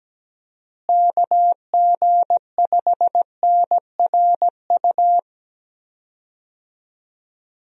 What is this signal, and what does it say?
Morse code, heard as a keyed tone, about 17 wpm, 700 Hz. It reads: KG5NRU
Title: USA Call Signs - 6 Characters - Repeated 3x - 17wpm